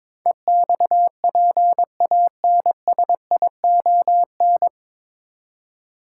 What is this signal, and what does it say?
Morse code, heard as a keyed tone, about 22 wpm, 700 Hz. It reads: EXPANSION
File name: 9 Letter Words - First 3 Letters Given - 22wpm